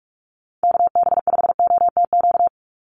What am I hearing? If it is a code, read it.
Morse code, heard as a keyed tone, about 44 wpm, 700 Hz. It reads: KB5OTQ
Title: USA Call Signs 45wpm